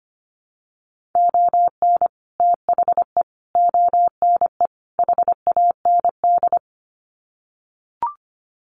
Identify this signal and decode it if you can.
Morse code, heard as a keyed tone, about 25 wpm, 700 Hz. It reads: ON THE ONE HAND
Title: Sets of 4 Words 25wpm